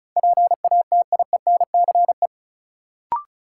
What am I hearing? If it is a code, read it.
Morse code, heard as a keyed tone, about 35 wpm, 700 Hz. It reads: PATIENCE
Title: Sentences from Top 1300 Words - New Word Drill Alt - 35wpm